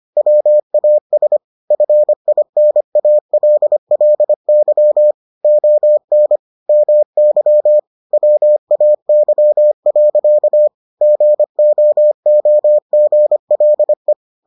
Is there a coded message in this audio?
Yes: Morse code from 0.2 to 14.1 s, heard as a keyed tone, about 25 wpm, 600 Hz. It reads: WAS FINALLY ON MY WAY. GOOGLE